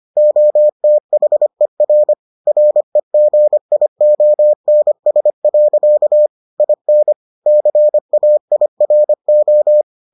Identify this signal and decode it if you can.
Morse code, heard as a keyed tone, about 25 wpm, 600 Hz. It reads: OTHER REGIONS. IN CAIRO